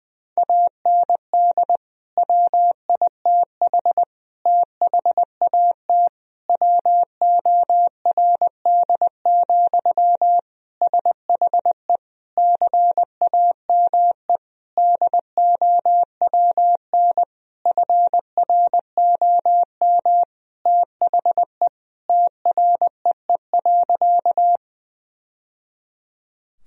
Morse code, heard as a keyed tone, about 20 wpm, 700 Hz. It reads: AND WITH THAT WORD, SHE CAME DOWN FROM THE TREE.
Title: Sentences from Top 1000 Words - Review Part 1 - 20wpm